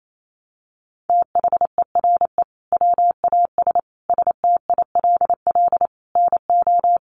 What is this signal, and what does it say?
Morse code, heard as a keyed tone, about 28 wpm, 700 Hz. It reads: THERE WAS STILL NO
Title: Sentences from Top 1400 Words 28wpm